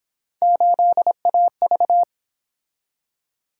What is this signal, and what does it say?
Morse code, heard as a keyed tone, about 27 wpm, 700 Hz. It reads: 8AV
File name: Speed-Racing - Three Letter-Number Combinations - 22wpm